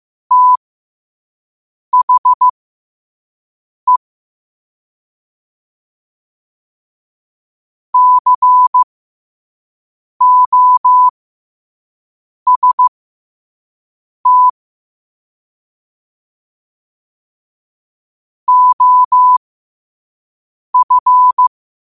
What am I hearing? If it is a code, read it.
Morse code, heard as a keyed tone, about 15 wpm, spaced as for 5 wpm, 1000 Hz. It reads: THE COST OF
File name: qotdfiles / 5wpm